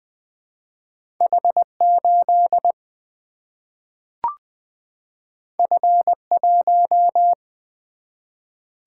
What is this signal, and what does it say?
Morse code, heard as a keyed tone, about 20 wpm, 700 Hz. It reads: H 8 F 1
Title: Single-Letter Single-Number 20wpm